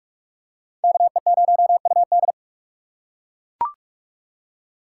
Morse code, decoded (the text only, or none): KE0UD